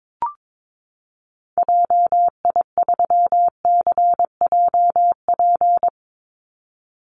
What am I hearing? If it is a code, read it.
Morse code, heard as a keyed tone, about 22 wpm, 690 Hz. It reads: JI3CJP
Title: Call Signs Repeated 2x - 22wpm